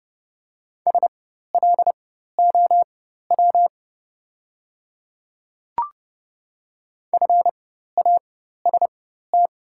Morse code, heard as a keyed tone, about 30 wpm, spaced as for 14 wpm, 690 Hz. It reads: SLOW FAST
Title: 4x Character Spacing - Top 500 Words - 30wpm